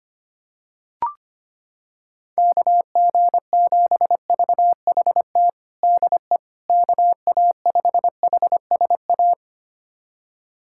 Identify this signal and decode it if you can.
Morse code, heard as a keyed tone, about 25 wpm, 700 Hz. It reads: KG7VHT DE KA5HSA
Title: USA Callsign DE Callsign - 6 Characters - 25wpm